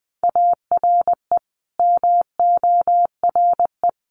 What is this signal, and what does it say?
Morse code, heard as a keyed tone, about 20 wpm, 700 Hz. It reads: ARE MORE
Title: Sets of 5 Words 20wpm